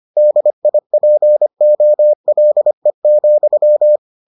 Morse code, heard as a keyed tone, about 25 wpm, 600 Hz. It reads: DIPOLE,